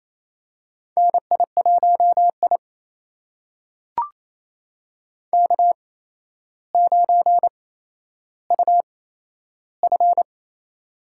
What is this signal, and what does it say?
Morse code, heard as a keyed tone, about 28 wpm, spaced as for 12 wpm, 700 Hz. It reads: NI1I K9UF
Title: USA Call Signs - 4 Characters - 8x Character Spacing - 28wpm